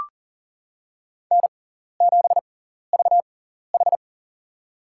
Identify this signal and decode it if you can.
Morse code, heard as a keyed tone, about 40 wpm, spaced as for 14 wpm, 700 Hz. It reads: N7VH
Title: USA Call Signs - 4 Characters - 6x Character Spacing - 40wpm